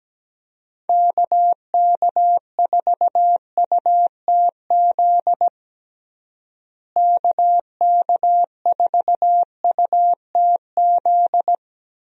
Morse code, heard as a keyed tone, about 17 wpm, 700 Hz. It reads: KK4UTZ KK4UTZ
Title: USA Call Signs - 6 Characters - Repeated 2x - 17wpm